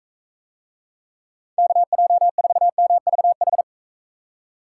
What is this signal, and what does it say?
Morse code, heard as a keyed tone, about 44 wpm, 700 Hz. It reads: KJ4MVH